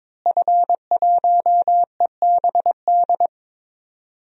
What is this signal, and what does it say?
Morse code, heard as a keyed tone, about 22 wpm, 700 Hz. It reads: F1EBD
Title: DXCC-100 Call Signs - 5 Characters - Repeated 3x - 22wpm